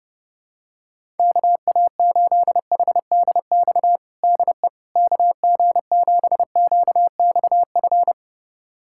Morse code, heard as a keyed tone, about 30 wpm, 700 Hz. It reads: KA8HDX DE KG7QXF
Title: USA Callsign DE Callsign - 6 Characters - 30wpm